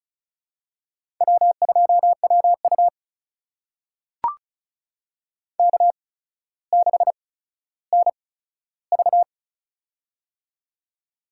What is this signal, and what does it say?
Morse code, heard as a keyed tone, about 36 wpm, spaced as for 15 wpm, 700 Hz. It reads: W2WU K6NV
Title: USA Call Signs - 4 Characters - 8x Character Spacing - 35wpm